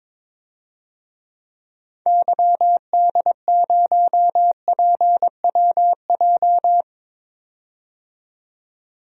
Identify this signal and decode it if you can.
Morse code, heard as a keyed tone, about 21 wpm, 700 Hz. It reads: YD0PWJ
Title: DXCC-100 Call Signs - 6 Characters - 22wpm